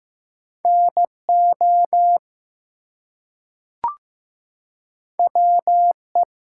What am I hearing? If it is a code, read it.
Morse code, heard as a keyed tone, about 15 wpm, 700 Hz. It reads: NO WE
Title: Two Letter Words - 15wpm